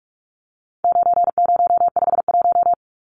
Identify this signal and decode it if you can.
Morse code, heard as a keyed tone, about 44 wpm, 700 Hz. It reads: 9051
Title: Numbers - 4-Digit 45wpm